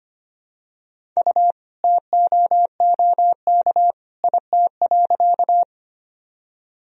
Morse code, heard as a keyed tone, about 25 wpm, 700 Hz. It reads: U TOOK IT.
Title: Sentences from Top 300 Words and Abbreviations 25wpm